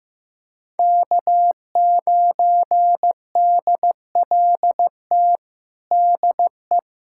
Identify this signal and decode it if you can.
Morse code, heard as a keyed tone, about 15 wpm, 700 Hz. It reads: K9DLT DE
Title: USA Callsign DE Callsign - 5 Characters - 15wpm